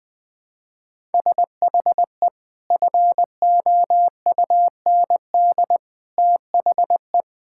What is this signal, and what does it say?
Morse code, heard as a keyed tone, about 21 wpm, 700 Hz. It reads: SHE FOUND THE
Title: Sentences from Top 1000 Words - Review Part 6 - 20wpm